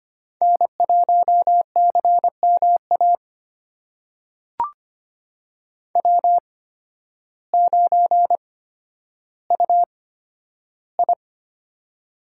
Morse code, heard as a keyed tone, about 25 wpm, spaced as for 12 wpm, 700 Hz. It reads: N1CMA W9UI
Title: USA Call Signs - 8x Character Spacing - 25wpm